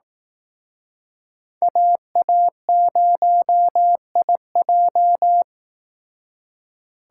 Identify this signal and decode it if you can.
Morse code, heard as a keyed tone, about 18 wpm, 700 Hz. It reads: AA0IJ
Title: Speed Racing - USA 2x2 Call Signs 15wpm